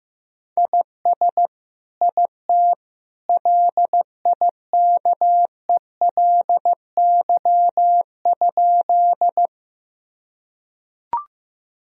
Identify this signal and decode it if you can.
Morse code, heard as a keyed tone, about 16 wpm, 700 Hz. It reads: IS IT LIKELY?
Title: Sentences from Top 1000 Words - Review Part 4 - 15wpm